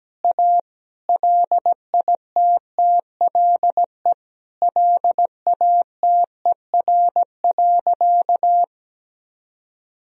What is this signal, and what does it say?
Morse code, heard as a keyed tone, about 17 wpm, 700 Hz. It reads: A LITTLE LATER.
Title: Sentences from Top 1000 Words - Review Part 3 - 17wpm